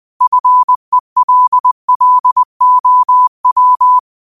Morse code, heard as a keyed tone, about 20 wpm, 1000 Hz. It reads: FELLOW
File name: qotdfiles / 20wpm